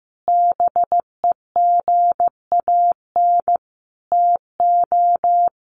Morse code, heard as a keyed tone, about 15 wpm, 700 Hz. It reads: BEGAN TO
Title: Sentences from Top 900 Words 15wpm